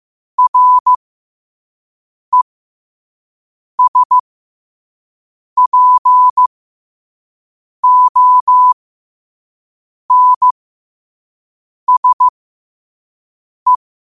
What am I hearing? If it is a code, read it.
Morse code, heard as a keyed tone, about 15 wpm, spaced as for 5 wpm, 1000 Hz. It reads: RESPONSE
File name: qotdfiles / 5wpm